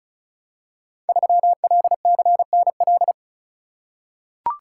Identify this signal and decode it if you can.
Morse code, heard as a keyed tone, about 36 wpm, 690 Hz. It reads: 3LCNL
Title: Code Groups - 5 Letter:Numbers - 35wpm